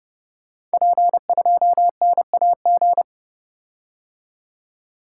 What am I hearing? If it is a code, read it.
Morse code, heard as a keyed tone, about 30 wpm, 700 Hz. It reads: P2NAG
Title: Code Groups - 5 Letter:Numbers - 30wpm